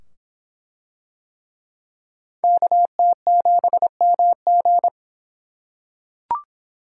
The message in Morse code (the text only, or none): KT7MG